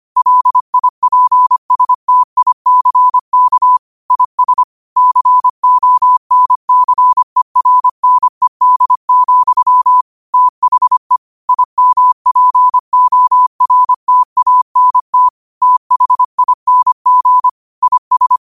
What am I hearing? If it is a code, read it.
Morse code, heard as a keyed tone, about 25 wpm, 1000 Hz. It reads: LIPSTICK IS CONCERNED, THE IMPORTANT THING IS
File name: qotdfiles / 25wpm